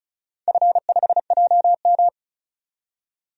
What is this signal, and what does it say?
Morse code, heard as a keyed tone, about 34 wpm, 700 Hz. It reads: F5JM
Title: DXCC-100 Call Signs - 4 Characters - 35wpm